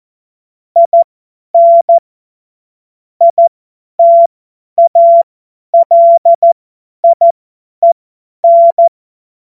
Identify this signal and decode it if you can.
Morse code, heard as a keyed tone, about 14 wpm, spaced as for 10 wpm, 680 Hz. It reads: IN ITALIEN
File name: Morsekurs / Witze / Witz 22. Juni 2021, 50 F70